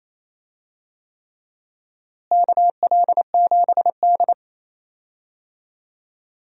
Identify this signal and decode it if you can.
Morse code, heard as a keyed tone, about 28 wpm, 700 Hz. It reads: KL7D
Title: DXCC-100 Call Signs - 4 Characters - Repeated 3x - 28wpm